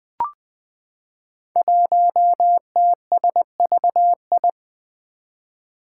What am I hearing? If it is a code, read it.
Morse code, heard as a keyed tone, about 20 wpm, 700 Hz. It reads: 1TSVI
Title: Five Challenging Letter-Number Combinations 20wpm